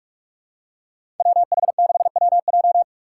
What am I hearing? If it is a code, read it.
Morse code, heard as a keyed tone, about 42 wpm, 700 Hz. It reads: WH6WJ
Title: USA Call Signs - 5 Characters - 6x Character Spacing - 45wpm